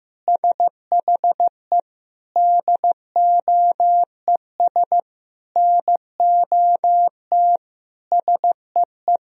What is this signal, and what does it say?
Morse code, heard as a keyed tone, about 15 wpm, 700 Hz. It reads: SHE DOES NOT SEE
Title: Sentences from Top 1000 Words - Review Part 3 - 15wpm